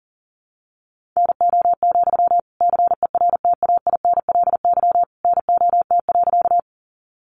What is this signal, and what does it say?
Morse code, heard as a keyed tone, about 40 wpm, 700 Hz. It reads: NO, CERTAINLY NOT.
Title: Sentences from Top 900 Words 40wpm